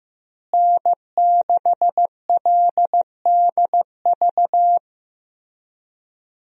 Morse code, heard as a keyed tone, about 15 wpm, 700 Hz. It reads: N6LDV
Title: USA Call Signs - 5 Characters - Repeated 2x - 15wpm